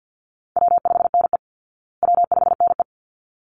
Morse code, heard as a keyed tone, about 50 wpm, 700 Hz. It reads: W5NE W5NE
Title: Easy Call Signs Repeated 2x - 50wpm